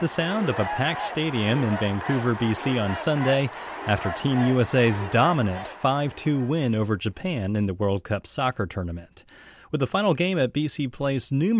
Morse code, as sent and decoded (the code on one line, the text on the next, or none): none